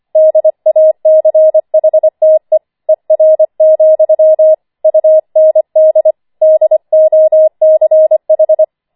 Morse code, heard as a keyed tone, about 25 wpm, 630 Hz. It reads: DACHTE ER, UND DOCH
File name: 25wpm